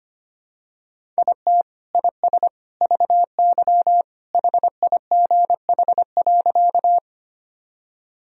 Morse code, heard as a keyed tone, about 25 wpm, 700 Hz. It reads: IT IS VY HIGH.
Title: Sentences from Top 200 Words and Abbreviations 25wpm